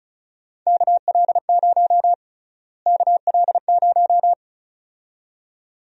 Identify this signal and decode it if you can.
Morse code, heard as a keyed tone, about 34 wpm, 700 Hz. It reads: KL0 KL0
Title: USA Call Sign Prefixes - Repeated 2x - 35wpm